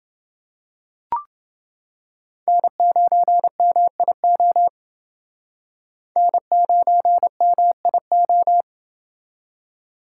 Morse code, heard as a keyed tone, about 29 wpm, 700 Hz. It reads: N9MIO N9MIO